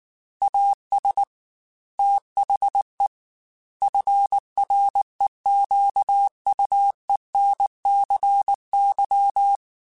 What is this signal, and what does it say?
Morse code, heard as a keyed tone, about 19 wpm, 800 Hz. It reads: AS THE FREQUENCY